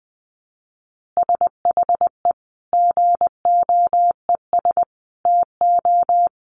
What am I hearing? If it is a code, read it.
Morse code, heard as a keyed tone, about 20 wpm, 700 Hz. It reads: SHE GOES TO
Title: Additional Easy Phrases 20wpm